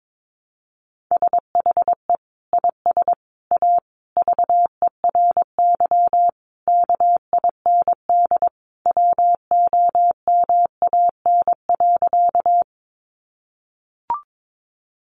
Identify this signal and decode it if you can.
Morse code, heard as a keyed tone, about 22 wpm, 700 Hz. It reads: SHE IS A VERY KIND WOMAN.
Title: Sentences from Top 1100 Words 22wpm